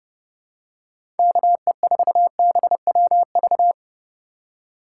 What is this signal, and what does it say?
Morse code, heard as a keyed tone, about 32 wpm, 700 Hz. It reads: KE4BWV
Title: USA Call Signs - 6 Characters - 3x Character Spacing - 30wpm